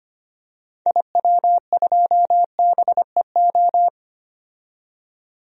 Morse code, heard as a keyed tone, about 25 wpm, 700 Hz. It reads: IW2BEO
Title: DXCC-100 Call Signs - 6 Characters - 25wpm